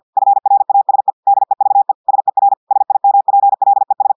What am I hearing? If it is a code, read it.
Morse code, heard as a keyed tone, about 50 wpm, 800 Hz. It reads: FRASE DEVE SER SIMPLES
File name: portugais texte en clair 2 vitesse 50 mots minutes